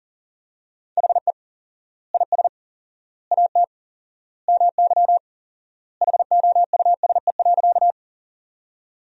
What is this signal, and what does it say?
Morse code, heard as a keyed tone, about 40 wpm, 700 Hz. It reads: HE IS AT MY HOUSE.
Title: 4x Word-Spacing - Sentences from Top 200 Words - 40wpm